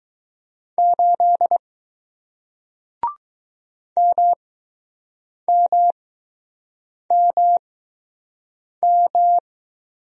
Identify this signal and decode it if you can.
Morse code, heard as a keyed tone, about 20 wpm, spaced as for 6 wpm, 700 Hz. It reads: 8 MMMM